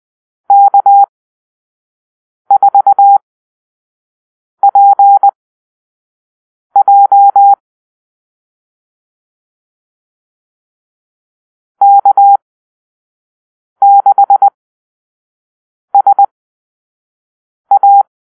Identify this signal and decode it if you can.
Morse code, heard as a keyed tone, about 20 wpm, spaced as for 5 wpm, 800 Hz. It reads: K4PJ K6SA
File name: arrl / Call-Sign Drill-1